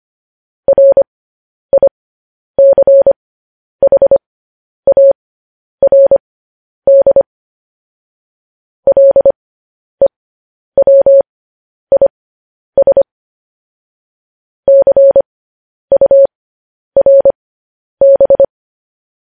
Morse code, heard as a keyed tone, about 25 wpm, spaced as for 10 wpm, 550 Hz. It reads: RICHARD LEWIS CURB